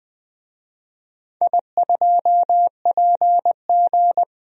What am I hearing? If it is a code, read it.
Morse code, heard as a keyed tone, about 20 wpm, 700 Hz. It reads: I2PG
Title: DXCC-100 Call Signs - 4 Characters - 20wpm